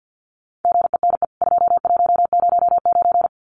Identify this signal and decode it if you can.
Morse code, heard as a keyed tone, about 49 wpm, 700 Hz. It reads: GENE 2109
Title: CWT Exchange 50wpm